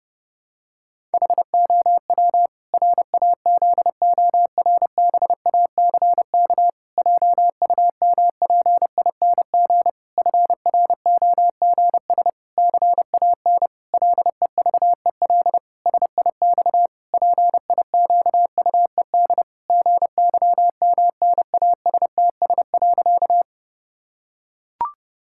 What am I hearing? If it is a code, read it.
Morse code, heard as a keyed tone, about 30 wpm, 700 Hz. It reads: HOW RAZORBACK JUMPING FROGS CAN LEVEL SIX PIQUED GYMNASTS.
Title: Pangram Challenge v2 - 30wpm